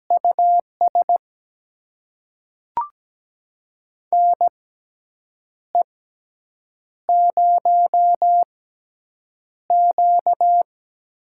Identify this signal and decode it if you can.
Morse code, heard as a keyed tone, about 17 wpm, spaced as for 7 wpm, 700 Hz. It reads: US NE0Q
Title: USA Call Signs - 6x Character Spacing - 17wpm